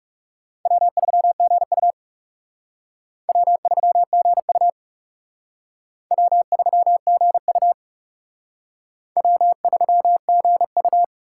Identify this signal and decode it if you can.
Morse code, heard as a keyed tone, about 36 wpm, 700 Hz. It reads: W3GU W3GU W3GU W3GU
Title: Speed Racing - USA 1x2 Call Signs 30wpm